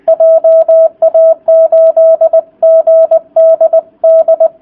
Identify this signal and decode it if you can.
Morse code, heard as a keyed tone, about 20 wpm, 650 Hz. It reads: JA8GDD